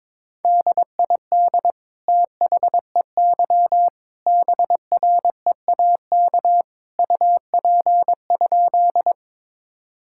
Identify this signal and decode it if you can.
Morse code, heard as a keyed tone, about 21 wpm, 700 Hz. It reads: DID THEY BREAK UP?
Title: Sentences from Top 1000 Words - Review Part 6 - 22wpm